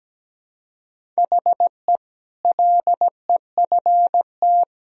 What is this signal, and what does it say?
Morse code, heard as a keyed tone, about 17 wpm, 700 Hz. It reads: HE LEFT
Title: Sentences from Top 1000 Words - Review Part 1 - 17wpm